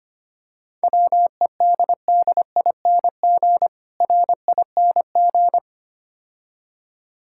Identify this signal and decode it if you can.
Morse code, heard as a keyed tone, about 25 wpm, 700 Hz. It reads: WEDDING RING
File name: practice / 25wpm / Difficult Words - Speed-Racing with Hints - 25wpm